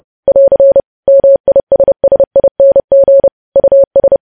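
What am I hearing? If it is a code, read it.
Morse code, heard as a keyed tone, about 30 wpm, 550 Hz. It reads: <AR> MISSING US
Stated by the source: News headlines sent in Morse code at 30.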